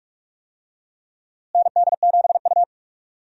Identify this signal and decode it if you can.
Morse code, heard as a keyed tone, about 46 wpm, 700 Hz. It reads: ND7U
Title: USA Call Signs - 4 Characters - Repeated 3x - 45wpm